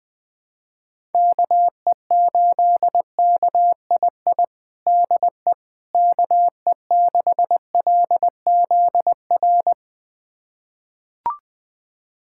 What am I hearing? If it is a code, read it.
Morse code, heard as a keyed tone, about 20 wpm, 700 Hz. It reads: KE8KII DE KE6LZR
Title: USA Callsign DE Callsign - 6 Characters - 20wpm